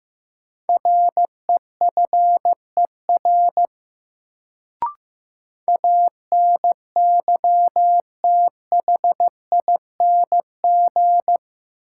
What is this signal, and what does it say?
Morse code, heard as a keyed tone, about 15 wpm, 700 Hz. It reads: REFER ANYTHING
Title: Sentences from Top 1000 Words - New Word Drill - 15wpm